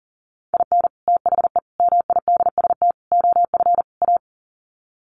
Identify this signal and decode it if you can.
Morse code, heard as a keyed tone, about 40 wpm, 700 Hz. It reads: IN THE MIDST OF A
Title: Sets of 5 Words 40wpm